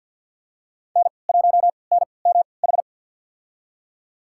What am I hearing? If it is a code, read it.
Morse code, heard as a keyed tone, about 51 wpm, spaced as for 28 wpm, 700 Hz. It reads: N1NMH
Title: USA Call Signs - 5 Characters - 3x Character Spacing - 50wpm